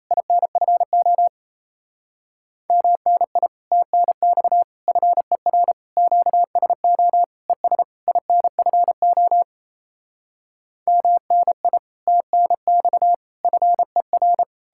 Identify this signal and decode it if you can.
Morse code, heard as a keyed tone, about 37 wpm, 700 Hz. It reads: INFO MNI TNX FER QSO ES INFO MNI TNX FER